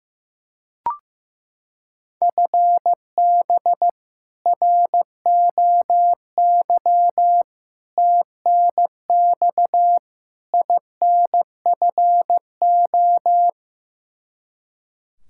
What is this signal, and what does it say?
Morse code, heard as a keyed tone, about 15 wpm, 700 Hz. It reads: FB ROY TNX INFO
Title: K1USN SST QSO Practice 15wpm